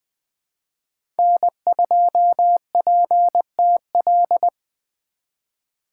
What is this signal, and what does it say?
Morse code, heard as a keyed tone, about 20 wpm, 700 Hz. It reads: N2PTL
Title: USA Call Signs - 2x Character Spacing - 20wpm